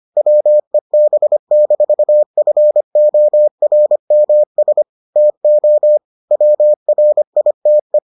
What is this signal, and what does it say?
Morse code, heard as a keyed tone, about 25 wpm, 600 Hz. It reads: WEB-FORMS TO WRITE